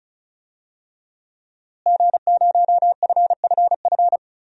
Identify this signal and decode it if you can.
Morse code, heard as a keyed tone, about 34 wpm, 700 Hz. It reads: G0FFF
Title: DXCC-100 Call Signs - 5 Characters - Repeated 3x - 35wpm